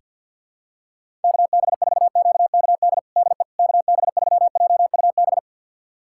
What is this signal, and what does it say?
Morse code, heard as a keyed tone, about 49 wpm, 690 Hz. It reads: KB4QKD DE KB3JUB
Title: USA Callsign DE Callsign - 6 Characters - 50wpm